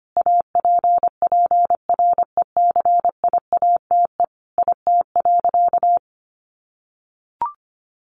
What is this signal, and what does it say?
Morse code, heard as a keyed tone, about 25 wpm, 700 Hz. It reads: APPRECIATE IT.
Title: Sentences from Top 1300 Words 25wpm